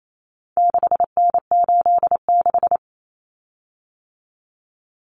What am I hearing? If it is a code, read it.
Morse code, heard as a keyed tone, about 28 wpm, 700 Hz. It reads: 6N86
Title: Common Cut Numbers - 4-Digit 28wpm